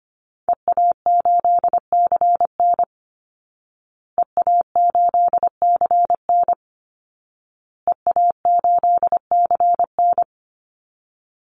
Morse code, heard as a keyed tone, about 25 wpm, 700 Hz. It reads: EA8CN EA8CN EA8CN
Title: Call Signs Repeated 2x - 25wpm